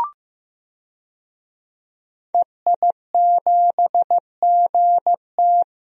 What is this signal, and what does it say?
Morse code, heard as a keyed tone, about 15 wpm, 700 Hz. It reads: EI7GT
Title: DXCC-100 Call Signs - 5 Characters - Repeated 3x - 15wpm